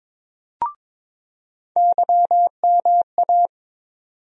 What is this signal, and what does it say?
Morse code, heard as a keyed tone, about 21 wpm, 700 Hz. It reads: YMA
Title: USA Call Sign Postfixes - 3 Letters 22wpm